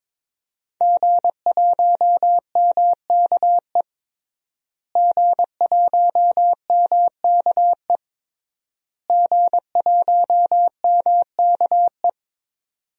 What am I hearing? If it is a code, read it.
Morse code, heard as a keyed tone, about 22 wpm, 700 Hz. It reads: G1MKE G1MKE G1MKE
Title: DXCC-100 Call Signs - 5 Characters - Repeated 3x - 22wpm